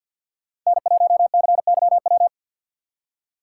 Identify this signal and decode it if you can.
Morse code, heard as a keyed tone, about 53 wpm, 700 Hz. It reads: N1CYW
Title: USA Call Signs - 5 Characters - 2x Character Spacing - 50wpm